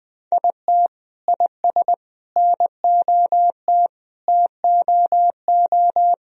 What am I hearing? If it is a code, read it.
Morse code, heard as a keyed tone, about 20 wpm, 700 Hz. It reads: IT IS NOT TOO
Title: Sentences from Top 1000 Words - Review Part 1 - 20wpm